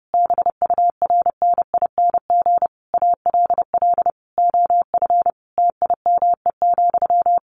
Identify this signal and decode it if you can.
Morse code, heard as a keyed tone, about 30 wpm, 700 Hz. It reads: BURNING ALL OF TIME,
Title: A Sound Of Thunder 30wpm